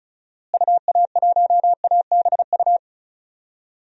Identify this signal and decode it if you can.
Morse code, heard as a keyed tone, about 36 wpm, 700 Hz. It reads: UA1ABU
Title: DXCC-100 Call Signs - 6 Characters - Repeated 2x - 35wpm